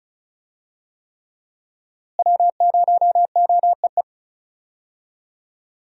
Morse code, heard as a keyed tone, about 36 wpm, 700 Hz. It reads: W0OEE
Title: USA Call Signs - 5 Characters - Repeated 3x - 35wpm